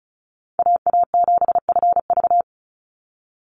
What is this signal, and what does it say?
Morse code, heard as a keyed tone, about 34 wpm, 700 Hz. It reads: AA7FV
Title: Call Signs - 35wpm